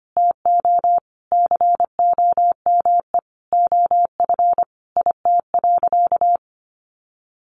Morse code, heard as a keyed tone, about 25 wpm, 700 Hz. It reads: TO COME OF IT.
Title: Sentences from Top 1100 Words 25wpm